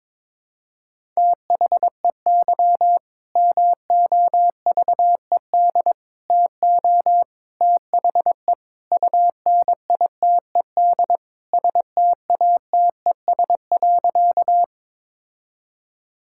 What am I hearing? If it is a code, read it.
Morse code, heard as a keyed tone, about 21 wpm, 700 Hz. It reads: THEY MOVED TO THE UNITED STATES.
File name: practice / 22wpm / Sentences with 4-Word Phrases - 22wpm